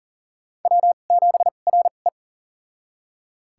Morse code, heard as a keyed tone, about 37 wpm, spaced as for 28 wpm, 700 Hz. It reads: W7RE
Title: USA Call Signs - 4 Characters - 2x Character Spacing - 40wpm